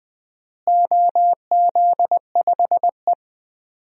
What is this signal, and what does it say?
Morse code, heard as a keyed tone, about 20 wpm, 700 Hz. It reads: OZ5E
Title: DXCC-100 Call Signs - 4 Characters - Repeated 3x - 20wpm